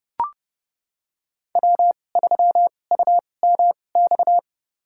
Morse code, heard as a keyed tone, about 32 wpm, spaced as for 22 wpm, 700 Hz. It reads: W3UMX
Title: USA Call Signs - 5 Characters - 2x Character Spacing - 30wpm